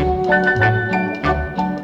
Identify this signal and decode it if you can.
There is no code.